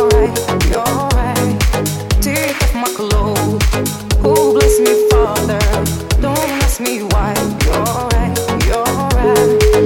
Ringback tone (Europe, 425 Hz), with noise.